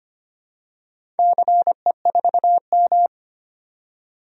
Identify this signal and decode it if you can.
Morse code, heard as a keyed tone, about 25 wpm, 700 Hz. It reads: CE4M
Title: DXCC-100 Call Signs - 4 Characters - 25wpm